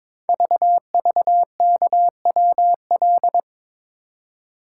Morse code, heard as a keyed tone, about 22 wpm, 700 Hz. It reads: VVKWL
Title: Code Groups - 5 Letters - 22wpm